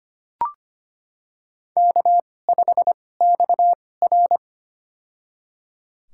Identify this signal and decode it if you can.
Morse code, heard as a keyed tone, about 25 wpm, spaced as for 18 wpm, 700 Hz. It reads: K5XR
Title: USA Call Signs - 4 Characters - 2x Character Spacing - 25wpm